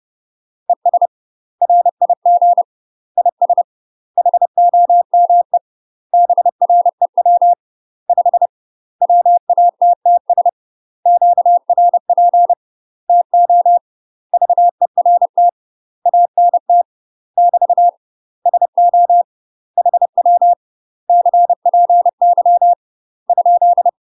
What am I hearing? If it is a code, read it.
Morse code, heard as a keyed tone, about 30 wpm, 700 Hz. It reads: ES RIG IS HOME BREW 5 WATTS QRP TO VERT ANT <BT> SO HW CPY ?
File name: UK MASTER QSOs TRAINING / 30WPM / UK MASTER QSOs 0026